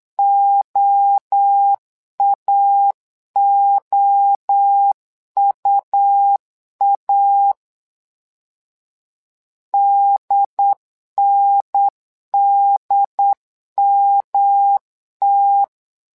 Morse code, heard as a keyed tone, about 8 wpm, 800 Hz. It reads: OAOUA DNDMT
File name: LEZIONI / 3 MWRO / 3 MWRO 8wpm 2606